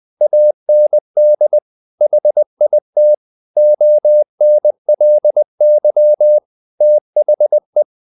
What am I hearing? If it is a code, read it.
Morse code, heard as a keyed tone, about 20 wpm, 600 Hz. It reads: AND HIT ONLY THE
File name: the old man and the sea part 2 20wpm